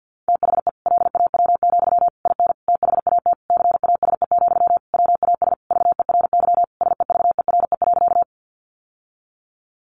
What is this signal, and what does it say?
Morse code, heard as a keyed tone, about 50 wpm, 700 Hz. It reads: THE LAW, IN THAT CASE, WAS VERY SEVERE.
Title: Sentences from Top 1400 Words 50wpm